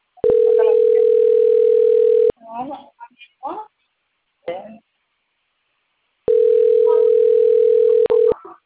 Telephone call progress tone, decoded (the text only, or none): ringback tone